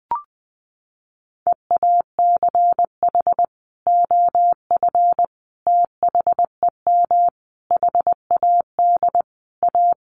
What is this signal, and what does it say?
Morse code, heard as a keyed tone, about 20 wpm, 700 Hz. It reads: EACH OF THEM HAD A
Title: Sentences from Top 1300 Words 20wpm